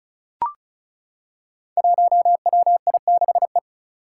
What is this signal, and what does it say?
Morse code, heard as a keyed tone, about 36 wpm, 700 Hz. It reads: 1WI6E
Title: Five Challenging Letter-Number Combinations 35wpm